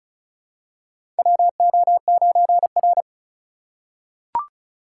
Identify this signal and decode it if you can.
Morse code, heard as a keyed tone, about 34 wpm, 700 Hz. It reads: WO9R
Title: USA Call Signs - 4 Characters - 4x Character Spacing - 35wpm